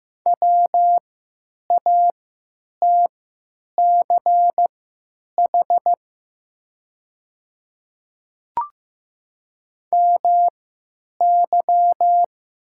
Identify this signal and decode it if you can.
Morse code, heard as a keyed tone, about 15 wpm, spaced as for 9 wpm, 690 Hz. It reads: WATCH MY
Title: 3x Character Spacing - Top 300 Words - 15wpm